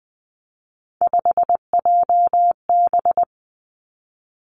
Morse code, heard as a keyed tone, about 20 wpm, 700 Hz. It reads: 5JB
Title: Three Challenging Letter-Number Combinations 20wpm